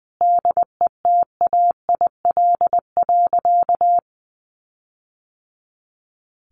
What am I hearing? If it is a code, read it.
Morse code, heard as a keyed tone, about 20 wpm, 700 Hz. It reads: DETAIL.
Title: Sentences from Top 1500 Words 20wpm